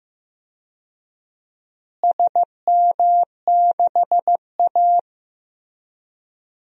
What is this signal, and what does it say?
Morse code, heard as a keyed tone, about 15 wpm, 700 Hz. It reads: SM6A
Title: DXCC-100 Call Signs - 4 Characters - Repeated 2x - 15wpm